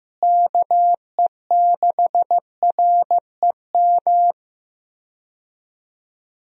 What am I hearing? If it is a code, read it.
Morse code, heard as a keyed tone, about 16 wpm, 700 Hz. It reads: KE6REM
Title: USA Call Signs - 6 Characters - Repeated 2x - 15wpm